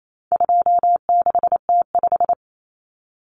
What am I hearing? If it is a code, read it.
Morse code, heard as a keyed tone, about 28 wpm, 700 Hz. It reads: 26T5
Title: Common Cut Numbers - 4-Digit 28wpm